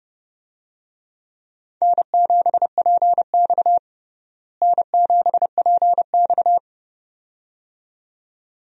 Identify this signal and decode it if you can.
Morse code, heard as a keyed tone, about 30 wpm, 700 Hz. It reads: N7PX N7PX
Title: USA Call Signs - 4 Characters - Repeated 2x - 30wpm